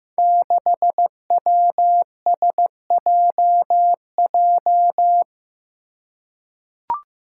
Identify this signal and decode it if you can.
Morse code, heard as a keyed tone, about 15 wpm, 700 Hz. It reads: 6WSJJ
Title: Five Challenging Letter-Number Combinations 15wpm